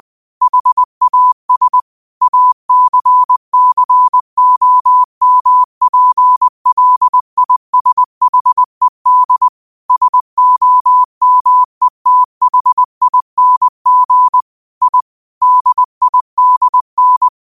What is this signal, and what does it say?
Morse code, heard as a keyed tone, about 20 wpm, 1000 Hz. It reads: HAS ACCOMPLISHED SOMETHING I DIDN
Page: Quotes for Fri, 15 Aug 2025 in Morse Code at 20 words per minute.